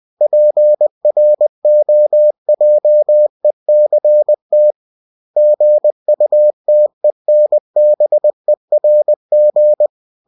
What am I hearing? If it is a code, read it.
Morse code, heard as a keyed tone, about 20 wpm, 600 Hz. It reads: PROJECT GUTENBERG
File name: cw-books / winnie-the-pooh / winnie-20wpm-page-0117